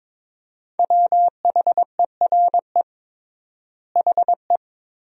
Morse code, heard as a keyed tone, about 22 wpm, 700 Hz. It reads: WHERE HE
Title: 3x Word-Spacing - Sentences from Top 400 Words - 22wpm